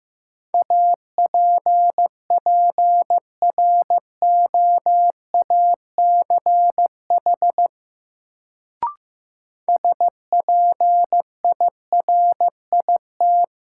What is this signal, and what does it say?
Morse code, heard as a keyed tone, about 15 wpm, 700 Hz. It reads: APPROACH SPIRIT
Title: Sentences from Top 1400 Words - New Word Drill Alt - 15wpm